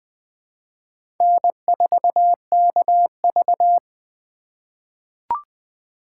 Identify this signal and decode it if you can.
Morse code, heard as a keyed tone, about 20 wpm, 700 Hz. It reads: N4KV
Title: USA Call Signs - 4 Characters - 3x Character Spacing - 20wpm